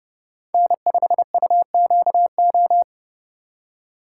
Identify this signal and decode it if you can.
Morse code, heard as a keyed tone, about 32 wpm, 700 Hz. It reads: N5UQO
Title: USA Call Signs - 5 Characters - 3x Character Spacing - 30wpm